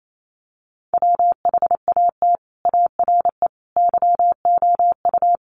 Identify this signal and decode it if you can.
Morse code, heard as a keyed tone, about 28 wpm, 700 Hz. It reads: WHAT ARE YOU
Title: Sets of 3 Words 28wpm